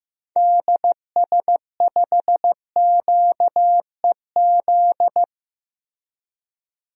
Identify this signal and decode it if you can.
Morse code, heard as a keyed tone, about 15 wpm, 700 Hz. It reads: DS5QEZ
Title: DXCC-100 Call Signs - 6 Characters - 15wpm